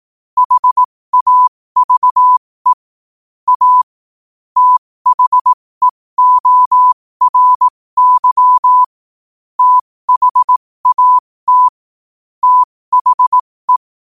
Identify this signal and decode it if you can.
Morse code, heard as a keyed tone, about 18 wpm, spaced as for 15 wpm, 1000 Hz. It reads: HAVE A THEORY THAT THE